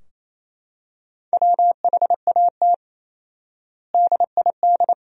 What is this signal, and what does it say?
Morse code, heard as a keyed tone, about 28 wpm, 700 Hz. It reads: WHAT DID
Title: 4x Word-Spacing - Sentences from Top 500 Words - 28wpm